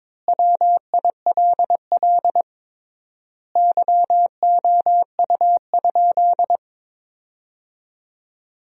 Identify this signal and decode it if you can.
Morse code, heard as a keyed tone, about 22 wpm, 700 Hz. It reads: WILL YOU?
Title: 3x Word-Spacing - Sentences from Top 300 Words - 22wpm